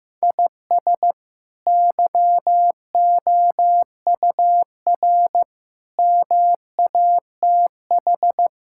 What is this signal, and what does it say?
Morse code, heard as a keyed tone, about 15 wpm, 700 Hz. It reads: IS YOUR MATH
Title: Sentences from Top 1000 Words - Review Part 6 - 15wpm